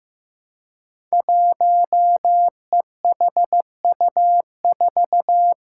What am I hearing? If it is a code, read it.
Morse code, heard as a keyed tone, about 15 wpm, 700 Hz. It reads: 1EHU4
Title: Five Challenging Letter-Number Combinations 15wpm